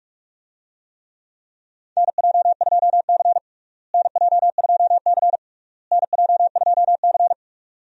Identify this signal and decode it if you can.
Morse code, heard as a keyed tone, about 41 wpm, 700 Hz. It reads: NJ2C NJ2C NJ2C
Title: USA Call Signs - 4 Characters - Repeated 3x - 45wpm